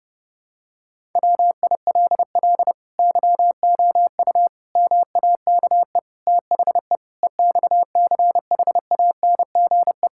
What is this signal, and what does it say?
Morse code, heard as a keyed tone, about 30 wpm, 700 Hz. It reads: WILL YOU MAKE THE EXCHANGE
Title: Sentences from Top 1000 Words - Review Part 6 - 30wpm